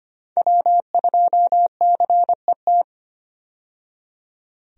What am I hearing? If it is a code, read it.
Morse code, heard as a keyed tone, about 25 wpm, 700 Hz. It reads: W2CET
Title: USA Call Signs - 5 Characters - 25wpm